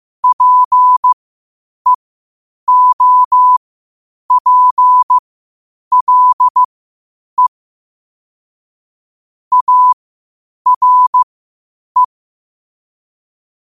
Morse code, heard as a keyed tone, about 15 wpm, spaced as for 8 wpm, 1000 Hz. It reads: PEOPLE ARE
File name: qotdfiles / 8wpm